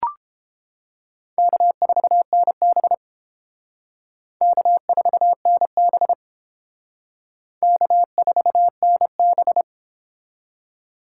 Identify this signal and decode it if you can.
Morse code, heard as a keyed tone, about 28 wpm, 700 Hz. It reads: K4NB K4NB K4NB